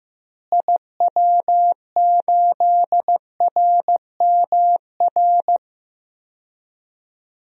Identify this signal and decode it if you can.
Morse code, heard as a keyed tone, about 15 wpm, 700 Hz. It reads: IW8RMR
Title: DXCC-100 Call Signs - 6 Characters - Repeated 3x - 15wpm